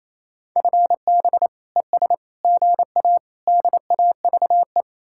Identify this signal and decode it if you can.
Morse code, heard as a keyed tone, about 36 wpm, 700 Hz. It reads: FB ES GA DAVE